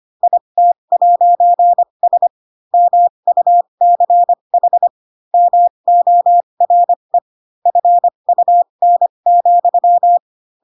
Morse code, heard as a keyed tone, about 25 wpm, spaced as for 22 wpm, 700 Hz. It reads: IT'S MUCH MORE FUN,
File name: Pinocchio / Pinocchio ch4 18 25 ik1hge